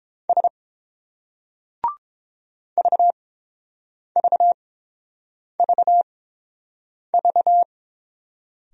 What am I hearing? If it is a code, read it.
Morse code, heard as a keyed tone, about 30 wpm, spaced as for 7 wpm, 690 Hz. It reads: S VVVV